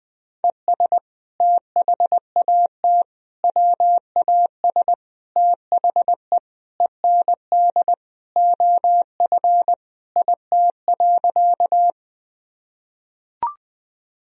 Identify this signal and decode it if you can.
Morse code, heard as a keyed tone, about 20 wpm, 700 Hz. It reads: ES THAT WAS THE END OF IT.
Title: Sentences from Top 200 Words and Abbreviations 20wpm